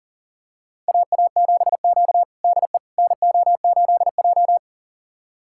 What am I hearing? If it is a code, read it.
Morse code, heard as a keyed tone, about 40 wpm, 700 Hz. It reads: AA7Q DE NO8J
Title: USA Callsign DE Callsign - 4 Characters - 40wpm